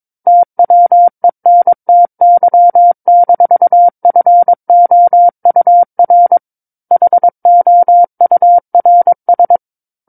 Morse code, heard as a keyed tone, about 22 wpm, 700 Hz. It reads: TWENTY-FOUR HOURS
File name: Book One / 13-Chapter 13 - 22 WPM